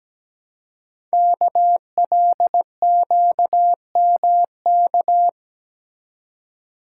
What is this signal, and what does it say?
Morse code, heard as a keyed tone, about 17 wpm, 700 Hz. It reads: KLQMK
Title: Code Groups - 5 Letters - 17wpm